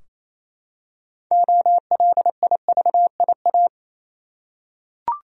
Morse code, heard as a keyed tone, about 27 wpm, 700 Hz. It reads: OLIVIA
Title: USA Common Names - 28wpm